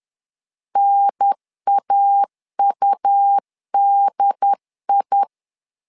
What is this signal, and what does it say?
Morse code, heard as a keyed tone, about 11 wpm, 800 Hz. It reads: NAUDI
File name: LEZIONI / 3 MWRO / 3 M 10wpm 1828